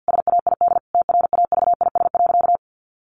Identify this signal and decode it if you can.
Morse code, heard as a keyed tone, about 50 wpm, 700 Hz. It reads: SAID TRAVIS.
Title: A Sound Of Thunder 50wpm